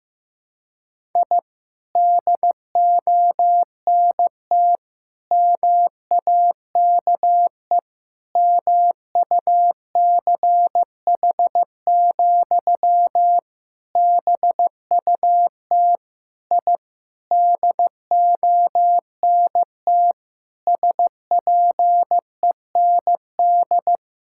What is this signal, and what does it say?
Morse code, heard as a keyed tone, about 15 wpm, 700 Hz. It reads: I DONT MAKE MUCH, BUT I DONT SPEND
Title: Sentences from Top 1000 Words - Review Part 5 - 15wpm